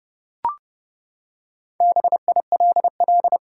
Morse code, heard as a keyed tone, about 30 wpm, 700 Hz. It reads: BILL
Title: Four Letter Words - 30wpm